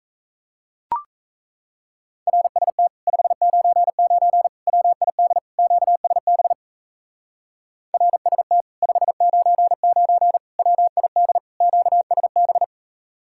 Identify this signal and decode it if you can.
Morse code, heard as a keyed tone, about 37 wpm, 700 Hz. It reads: RST 599 WID QSB RST 599 WID QSB